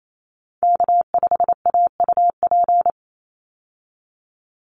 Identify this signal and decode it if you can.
Morse code, heard as a keyed tone, about 28 wpm, 700 Hz. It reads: K5AUP
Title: Call Signs Repeated 3x - 28wpm